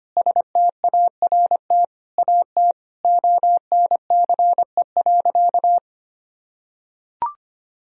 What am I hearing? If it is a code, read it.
Morse code, heard as a keyed tone, about 25 wpm, 700 Hz. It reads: START AT ONCE.
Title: Sentences from Top 1000 Words - Review Part 1 - 25wpm